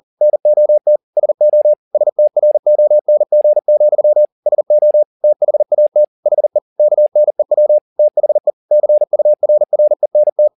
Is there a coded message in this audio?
Yes: Morse code from 0.2 to 10.5 s, heard as a keyed tone, about 40 wpm, 600 Hz. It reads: NOT SO STRONG, SO THAT HE KNEW THE CURRENT